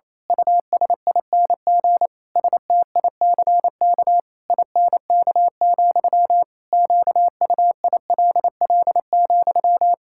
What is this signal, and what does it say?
Morse code, heard as a keyed tone, about 27 wpm, 700 Hz. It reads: USING STICK INK, QUILL,
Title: Pangram Challenge v2 - 28wpm